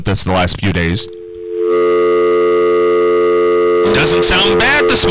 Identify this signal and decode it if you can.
Telephone call progress tone (North America), heard as a telephone network's tone, about 350, 440 Hz, with speech. Dial tone.